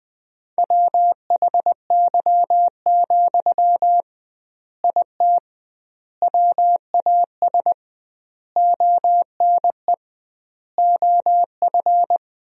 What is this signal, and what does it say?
Morse code, heard as a keyed tone, about 20 wpm, 700 Hz. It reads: WHY, IT WAS ONE OF
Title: 2x Word-Spacing - Sentences from Top 300 Words - 20wpm